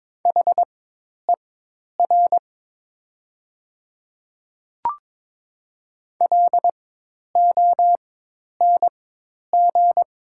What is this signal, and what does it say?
Morse code, heard as a keyed tone, about 22 wpm, spaced as for 10 wpm, 700 Hz. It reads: HER LONG
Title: 4x Character Spacing - Top 200 Words - 22wpm